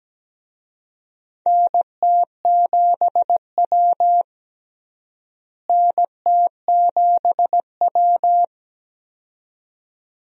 Morse code, heard as a keyed tone, about 17 wpm, 700 Hz. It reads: NT7W NT7W
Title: USA Call Signs - 4 Characters - Repeated 2x - 17wpm